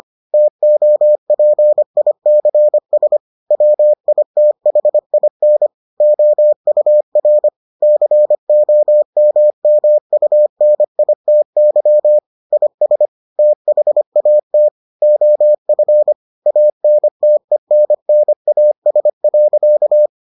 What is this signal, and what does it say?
Morse code, heard as a keyed tone, about 25 wpm, 600 Hz. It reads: TOPICS WITHIN OUR COMMUNITY IS THAT OF ANTENNAS.